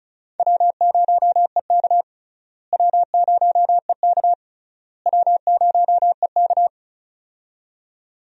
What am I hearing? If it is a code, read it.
Morse code, heard as a keyed tone, about 36 wpm, 700 Hz. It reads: W0EK W0EK W0EK
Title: USA Call Signs - 4 Characters - Repeated 3x - 35wpm